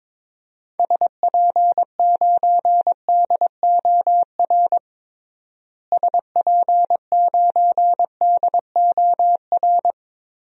Morse code, heard as a keyed tone, about 22 wpm, 700 Hz. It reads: SP9DOR SP9DOR
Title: DXCC-100 Call Signs - 6 Characters - Repeated 3x - 22wpm